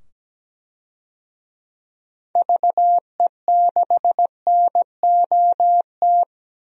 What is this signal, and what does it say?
Morse code, heard as a keyed tone, about 17 wpm, 700 Hz. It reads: VE6NOT
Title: DXCC-100 Call Signs - 6 Characters - 17wpm